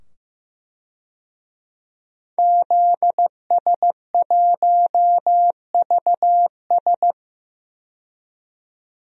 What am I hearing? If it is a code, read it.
Morse code, heard as a keyed tone, about 16 wpm, 700 Hz. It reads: ZS1VS
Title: DXCC-100 Call Signs - 5 Characters - 15wpm